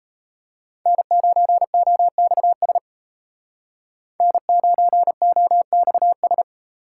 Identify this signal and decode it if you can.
Morse code, heard as a keyed tone, about 33 wpm, 700 Hz. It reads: N9OXS N9OXS
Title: Speed Racing - USA 1x3 Call Signs 28wpm